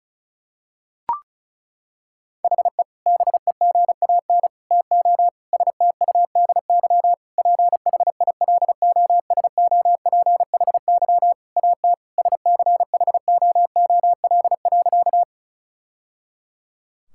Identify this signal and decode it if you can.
Morse code, heard as a keyed tone, about 36 wpm, 700 Hz. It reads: HE BEGAN TO STUDY PHILOSOPHY AT SCHOOL.
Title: Sentences from Top 1000 Words - Review Part 6 - 35wpm